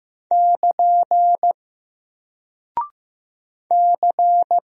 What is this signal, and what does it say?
Morse code, heard as a keyed tone, about 15 wpm, 700 Hz. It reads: <KN> C
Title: Single Character (Letter:Number:Punctuation:Prosign) Alt 15wpm